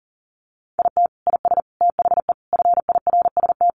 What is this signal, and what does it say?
Morse code, heard as a keyed tone, about 40 wpm, 700 Hz. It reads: IT IS THE FIRST
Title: Additional Easy Phrases 40wpm